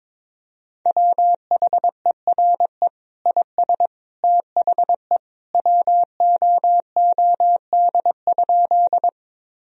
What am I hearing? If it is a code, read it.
Morse code, heard as a keyed tone, about 22 wpm, 700 Hz. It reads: WHERE IS THE WOOD?
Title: Sentences from Top 1000 Words - Review Part 2 - 22wpm